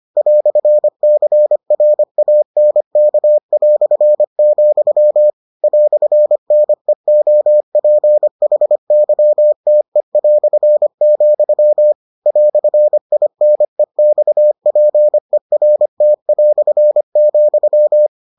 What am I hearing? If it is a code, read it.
Morse code, heard as a keyed tone, about 25 wpm, 600 Hz. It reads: "CRANK", "NEOPHYTE", "INEXPERT",